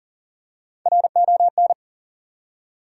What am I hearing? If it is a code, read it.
Morse code, heard as a keyed tone, about 41 wpm, 700 Hz. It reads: RON
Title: USA Common Names - Max 4 Letters - 40wpm